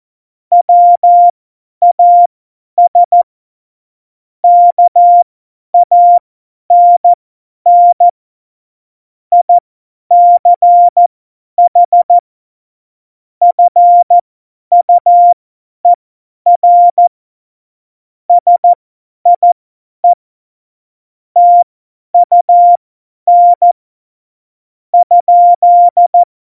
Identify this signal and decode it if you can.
Morse code, heard as a keyed tone, about 14 wpm, spaced as for 10 wpm, 700 Hz. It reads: WAS KANN ICH FUER SIE TUN ?
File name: Morsekurs / Witze / Witz 27 March 2018, 50 F70